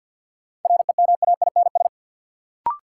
Morse code, heard as a keyed tone, about 47 wpm, 700 Hz. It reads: REMAINS
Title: Sets of 5 Words Encore - New Word Drill - 50wpm